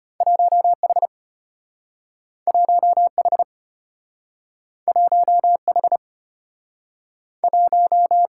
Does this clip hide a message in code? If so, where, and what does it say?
Yes: Morse code from 0.2 to 8.3 s, heard as a keyed tone, about 33 wpm, 700 Hz. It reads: 1H 1H 1H 1